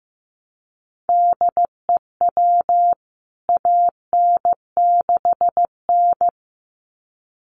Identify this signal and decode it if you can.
Morse code, heard as a keyed tone, about 15 wpm, 700 Hz. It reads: DEW AN6N
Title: CWT Exchange 15wpm